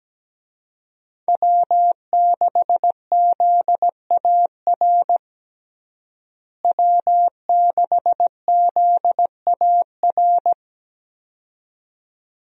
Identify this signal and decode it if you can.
Morse code, heard as a keyed tone, about 17 wpm, 700 Hz. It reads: W6ZAR W6ZAR
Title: USA Call Signs - 5 Characters - Repeated 2x - 17wpm